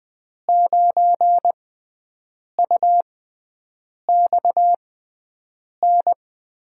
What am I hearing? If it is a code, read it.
Morse code, heard as a keyed tone, about 20 wpm, spaced as for 7 wpm, 700 Hz. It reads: 9UXN
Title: USA Call Signs - 5 Characters - 6x Character Spacing - 20wpm